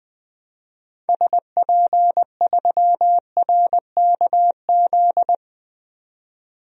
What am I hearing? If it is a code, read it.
Morse code, heard as a keyed tone, about 20 wpm, 700 Hz. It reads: SP3RKZ
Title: DXCC-100 Call Signs - 6 Characters - 20wpm